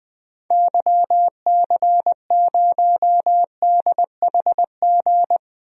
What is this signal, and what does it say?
Morse code, heard as a keyed tone, about 20 wpm, 700 Hz. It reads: YC0DHG
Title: DXCC-100 Call Signs - 6 Characters - 20wpm